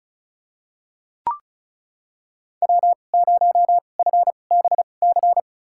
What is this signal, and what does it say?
Morse code, heard as a keyed tone, about 36 wpm, spaced as for 25 wpm, 700 Hz. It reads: W0FBC
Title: USA Call Signs - 5 Characters - 2x Character Spacing - 35wpm